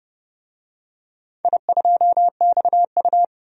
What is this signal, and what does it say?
Morse code, heard as a keyed tone, about 30 wpm, 700 Hz. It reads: I2XU
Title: DXCC-100 Call Signs - 4 Characters - 30wpm